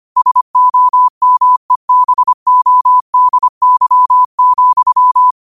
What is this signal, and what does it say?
Morse code, heard as a keyed tone, about 25 wpm, 1000 Hz. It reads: SOMEBODY,
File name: qotdfiles / 25wpm